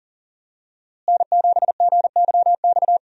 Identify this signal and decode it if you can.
Morse code, heard as a keyed tone, about 40 wpm, 700 Hz. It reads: N7GYX
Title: USA Call Signs - 5 Characters - Repeated 3x - 40wpm